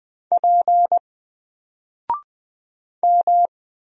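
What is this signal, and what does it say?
Morse code, heard as a keyed tone, about 20 wpm, 700 Hz. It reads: P M
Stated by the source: Single letter and number Morse drill at 20 wpm, from N4LCD.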